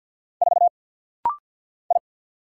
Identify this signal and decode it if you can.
Morse code, heard as a keyed tone, about 50 wpm, 700 Hz. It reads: 4 I
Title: Single Letter-Number - Rapid-Fire 50wpm